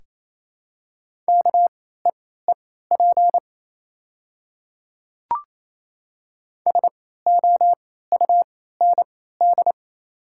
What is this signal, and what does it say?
Morse code, heard as a keyed tone, about 28 wpm, spaced as for 16 wpm, 700 Hz. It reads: KEEP SOUND
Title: 3x Character Spacing - Top 300 Words - 28wpm